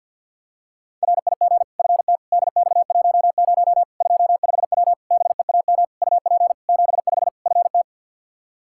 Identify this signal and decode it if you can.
Morse code, heard as a keyed tone, about 46 wpm, 700 Hz. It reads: RIG FT DX10 25W BEAM UP 75 FT
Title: Practice QSO - Exchange 2 - Info - 50wpm